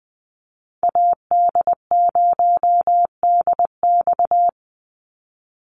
Morse code, heard as a keyed tone, about 20 wpm, 700 Hz. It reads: AD0DX
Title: Call Signs Repeated 2x - 20wpm